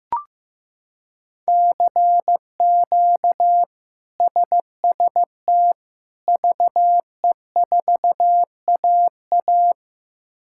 Morse code, heard as a keyed tone, about 16 wpm, 700 Hz. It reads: CQ SST VE4AA
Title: K1USN SST QSO Practice - 15wpm